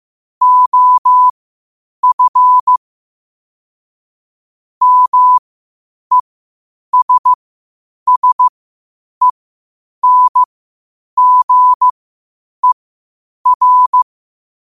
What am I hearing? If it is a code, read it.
Morse code, heard as a keyed tone, about 15 wpm, spaced as for 8 wpm, 1000 Hz. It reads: OF MESSENGER
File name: qotdfiles / 8wpm